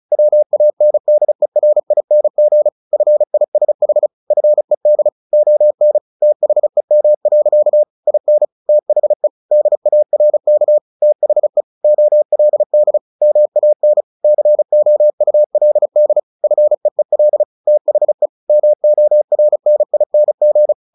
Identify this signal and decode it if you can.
Morse code, heard as a keyed tone, about 34 wpm, 600 Hz. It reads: WANDERING FISH FED ON THEM. IN THE DARK THE OLD MAN COULD FEEL THE MORNING
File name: the old man and the sea / the old man and the sea 1 35wpm